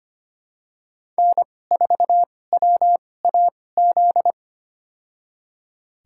Morse code, heard as a keyed tone, about 25 wpm, spaced as for 18 wpm, 700 Hz. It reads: N4WAZ
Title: USA Call Signs - 5 Characters - 2x Character Spacing - 25wpm